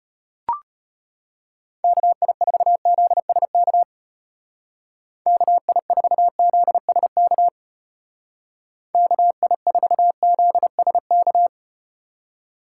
Morse code, heard as a keyed tone, about 34 wpm, 700 Hz. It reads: KI4ZSK KI4ZSK KI4ZSK